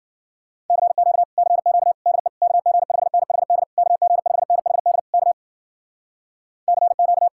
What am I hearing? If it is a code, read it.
Morse code, heard as a keyed tone, about 53 wpm, 700 Hz. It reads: CQ CQ DE KG5NHD KG5NHD K CQ